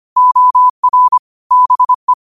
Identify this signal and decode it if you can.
Morse code, heard as a keyed tone, about 25 wpm, 1000 Hz. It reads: OR BE
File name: qotdfiles / 25wpm